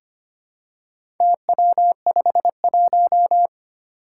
Morse code, heard as a keyed tone, about 25 wpm, 700 Hz. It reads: TW51
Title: Four Challenging Letter-Number Combinations 25wpm